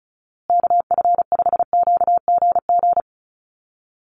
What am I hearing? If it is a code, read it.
Morse code, heard as a keyed tone, about 34 wpm, 700 Hz. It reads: KF5QGG
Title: USA Call Signs Repeated 2x 35wpm